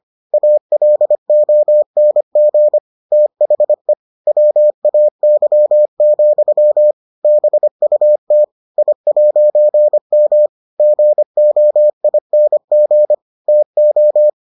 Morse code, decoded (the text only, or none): ALONG THE WAY, BUT I'M GOING TO